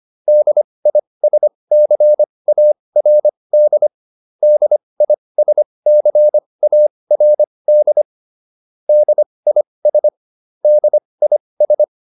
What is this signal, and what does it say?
Morse code, heard as a keyed tone, about 25 wpm, spaced as for 20 wpm, 600 Hz. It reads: DISCARD DISCARD DIS DIS
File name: DIS7 20